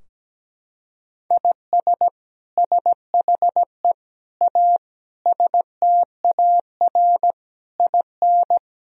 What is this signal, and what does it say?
Morse code, heard as a keyed tone, about 17 wpm, 700 Hz. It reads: IS SHE A STAR IN
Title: Sentences from Top 1000 Words - Review Part 2 - 17wpm